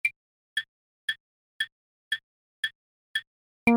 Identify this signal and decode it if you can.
There is no code.